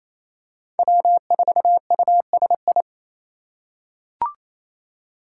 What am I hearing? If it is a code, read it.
Morse code, heard as a keyed tone, about 28 wpm, 700 Hz. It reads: W4USI
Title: K1USN SST QSO Practice - 28wpm